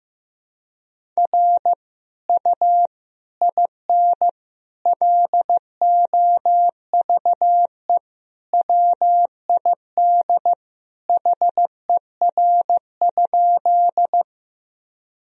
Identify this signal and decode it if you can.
Morse code, heard as a keyed tone, about 16 wpm, 700 Hz. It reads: R U IN LOVE WID HER?
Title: Sentences from Top 400 Words and Abbreviations 15wpm